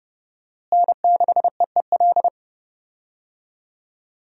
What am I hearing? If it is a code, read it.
Morse code, heard as a keyed tone, about 30 wpm, 700 Hz. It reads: N6EEL
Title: USA Call Signs - 5 Characters - Repeated 2x - 30wpm